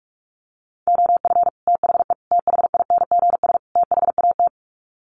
Morse code, heard as a keyed tone, about 44 wpm, 700 Hz. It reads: OF THE THINGS THAT
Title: Sets of 4 Words 45wpm